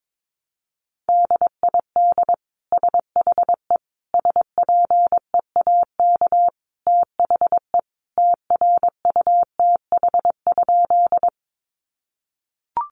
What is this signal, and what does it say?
Morse code, heard as a keyed tone, about 22 wpm, 700 Hz. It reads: DID SHE SPEAK THE TRUTH?
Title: Sentences from Top 1100 Words 22wpm